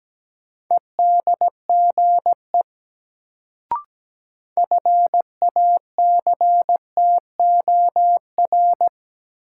Morse code, heard as a keyed tone, about 17 wpm, 700 Hz. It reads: EDGE FACTOR
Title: Sentences from Top 1300 Words - New Word Drill - 17wpm